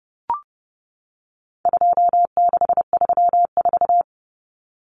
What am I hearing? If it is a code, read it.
Morse code, heard as a keyed tone, about 30 wpm, 700 Hz. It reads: 2634
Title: Numbers - 4-Digit 30wpm